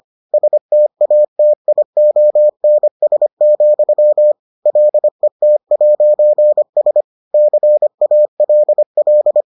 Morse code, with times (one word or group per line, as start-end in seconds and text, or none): 0.3-4.3 STATIONS,
4.7-7.0 LET'S
7.3-9.4 CALL